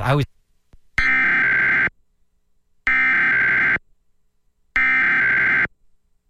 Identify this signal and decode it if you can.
Emergency Alert System (SAME) tones (header bursts), heard as harsh buzzing bursts, with speech. It is heard from 1.0 to 5.7 s.